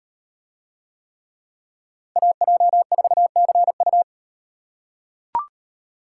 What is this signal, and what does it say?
Morse code, heard as a keyed tone, about 38 wpm, 700 Hz. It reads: AJ4CU